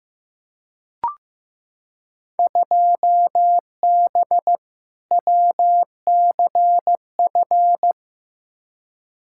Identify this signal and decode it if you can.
Morse code, heard as a keyed tone, about 15 wpm, 700 Hz. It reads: 2B WCF
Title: Field Day Exchanges Encore - 15wpm